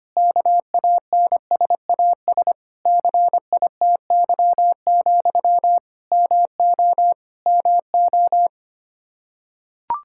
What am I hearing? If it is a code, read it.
Morse code, heard as a keyed tone, about 25 wpm, 700 Hz. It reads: KANSAS CITY, MO MO
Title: Practice QSO - Exchange 1 - Beginning - 25wpm